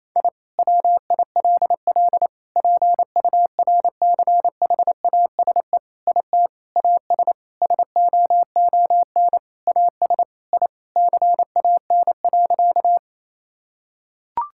Morse code, heard as a keyed tone, about 28 wpm, 700 Hz. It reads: I WILL PURCHASE IT AS SOON AS I CAN.
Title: Sentences from Top 1000 Words - Review Part 5 - 28wpm